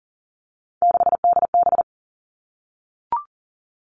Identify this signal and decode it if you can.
Morse code, heard as a keyed tone, about 40 wpm, 700 Hz. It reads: 6DB
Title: Three Challenging Letter-Number Combinations 40wpm